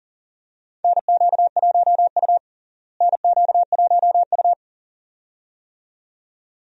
Morse code, heard as a keyed tone, about 40 wpm, 700 Hz. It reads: NQ1U NQ1U
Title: USA Call Signs - 4 Characters - Repeated 2x - 40wpm